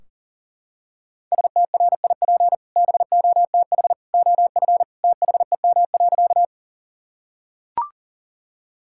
Morse code, heard as a keyed tone, about 37 wpm, 700 Hz. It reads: STRIP BOTH OF THEM.
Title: Sentences from Top 1600 Words 40wpm